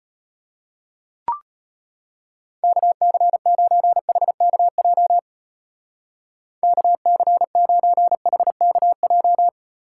Morse code, heard as a keyed tone, about 31 wpm, 700 Hz. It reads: KC9HKJ KC9HKJ
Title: Speed Racing - USA Call Signs 25wpm